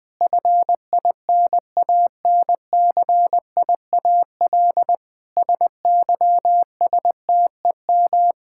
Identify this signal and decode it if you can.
Morse code, heard as a keyed tone, about 20 wpm, 700 Hz. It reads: FINANCIAL SYSTEM
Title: Sentences from Top 1000 Words - Review Part 5 - 20wpm